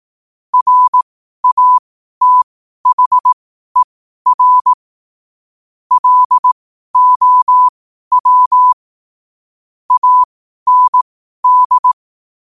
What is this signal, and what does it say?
Morse code, heard as a keyed tone, about 18 wpm, spaced as for 12 wpm, 1000 Hz. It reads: RATHER LOW AND
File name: qotdfiles / 12wpm